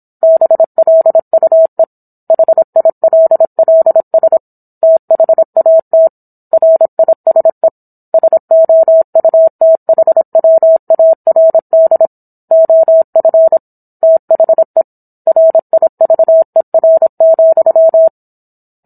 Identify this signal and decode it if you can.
Morse code, heard as a keyed tone, about 27 wpm, 650 Hz. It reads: BLUE HILLS THAT RISE SOUTHWARD OF THE RIVER,